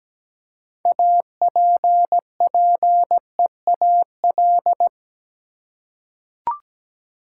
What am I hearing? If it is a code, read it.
Morse code, heard as a keyed tone, about 17 wpm, 700 Hz. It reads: APPEAL
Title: Six Letter Words - 17wpm